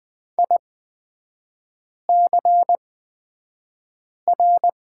Morse code, heard as a keyed tone, about 20 wpm, spaced as for 5 wpm, 690 Hz. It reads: ICR
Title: 229 Lesson 37e - Letter x - Three Characters 20wpm